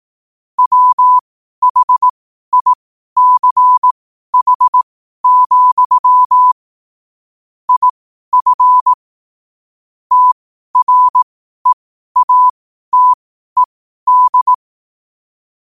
Morse code, heard as a keyed tone, about 18 wpm, spaced as for 12 wpm, 1000 Hz. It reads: WHICH, IF TREATED
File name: qotdfiles / 12wpm